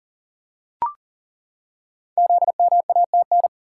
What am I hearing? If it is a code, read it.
Morse code, heard as a keyed tone, about 41 wpm, 700 Hz. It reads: ZMATN
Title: Code Groups - 5 Letters - 40wpm